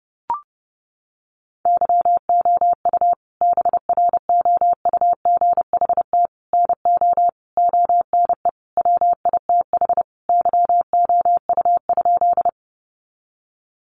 Morse code, heard as a keyed tone, about 30 wpm, 700 Hz. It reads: YOU BROUGHT NO ONE WITH YOU?
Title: Sentences from Top 500 Words 30wpm